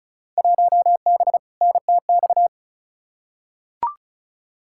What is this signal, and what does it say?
Morse code, heard as a keyed tone, about 34 wpm, 700 Hz. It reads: 1B NTX
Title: Field Day Exchanges - 35wpm